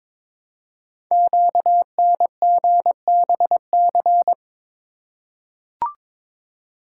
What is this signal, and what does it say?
Morse code, heard as a keyed tone, about 22 wpm, 700 Hz. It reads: QNGBC
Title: Code Groups - 5 Letters - 22wpm